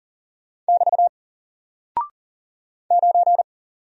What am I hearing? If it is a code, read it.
Morse code, heard as a keyed tone, about 40 wpm, 700 Hz. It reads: <BT> 9
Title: Single Character (Letter:Number:Punctuation:Prosign) Alt 40wpm